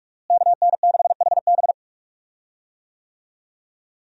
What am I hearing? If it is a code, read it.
Morse code, heard as a keyed tone, about 44 wpm, 700 Hz. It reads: KN6HB
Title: USA Call Signs - 5 Characters - Repeated 3x - 45wpm